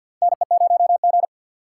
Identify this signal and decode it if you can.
Morse code, heard as a keyed tone, about 48 wpm, 700 Hz. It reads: NE0G
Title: USA Call Signs - 4 Characters - 6x Character Spacing - 50wpm